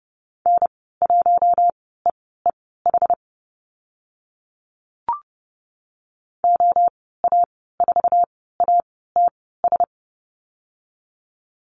Morse code, heard as a keyed tone, about 30 wpm, spaced as for 17 wpm, 700 Hz. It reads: N1EEH OA4ATS
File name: Lesson 14g - Number 2 - Call Signs 30wpm